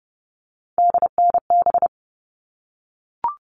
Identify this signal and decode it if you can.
Morse code, heard as a keyed tone, about 30 wpm, 700 Hz. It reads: DNB
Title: All Cut Numbers - 3-Digit 30wpm